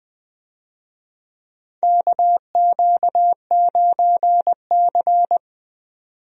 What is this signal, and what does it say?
Morse code, heard as a keyed tone, about 20 wpm, 700 Hz. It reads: KQ9C
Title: USA Call Signs - 4 Characters - 20wpm